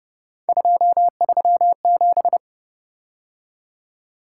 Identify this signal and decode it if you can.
Morse code, heard as a keyed tone, about 30 wpm, 700 Hz. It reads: 237
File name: Speed-Racing - Three Letter-Number Combinations - 25wpm